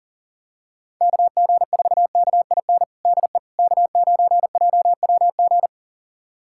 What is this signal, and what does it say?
Morse code, heard as a keyed tone, about 40 wpm, 700 Hz. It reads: KG4KIN DE K9JWG
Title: USA Callsign DE Callsign 40wpm